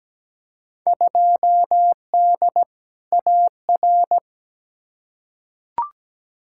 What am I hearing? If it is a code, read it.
Morse code, heard as a keyed tone, about 17 wpm, 700 Hz. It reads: 2D AR
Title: Field Day Exchanges Encore - 17wpm